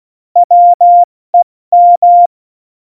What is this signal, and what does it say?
Morse code, heard as a keyed tone, about 16 wpm, 700 Hz. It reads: WEM
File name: Morsekurs / Witze / Witz 26. Januar 2021, 70 F80